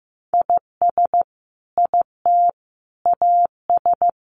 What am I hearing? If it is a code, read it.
Morse code, heard as a keyed tone, about 15 wpm, 700 Hz. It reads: IS IT AS
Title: Sentences from Top 200 Words 15wpm